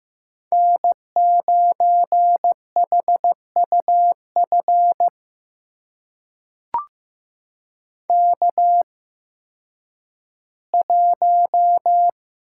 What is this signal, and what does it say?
Morse code, heard as a keyed tone, about 15 wpm, spaced as for 10 wpm, 700 Hz. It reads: N9HUF K1
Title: USA Call Signs - 5 Characters - 8x Character Spacing - 15wpm